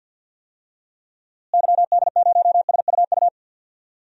Morse code, heard as a keyed tone, about 49 wpm, 700 Hz. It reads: YD0SUU
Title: DXCC-100 Call Signs - 6 Characters - Repeated 2x - 50wpm